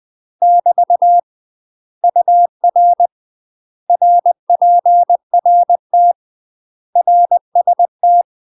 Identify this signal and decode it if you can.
Morse code, heard as a keyed tone, about 20 wpm, 700 Hz. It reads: <BT> UR RPRT RST
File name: UK MASTER QSOs TRAINING / 20WPM / UK MASTER QSOs 0039